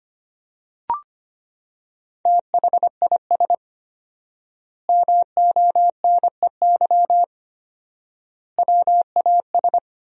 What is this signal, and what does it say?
Morse code, heard as a keyed tone, about 25 wpm, 700 Hz. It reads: THIS MONEY WAS
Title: 4x Word-Spacing - Sentences from Top 400 Words - 25wpm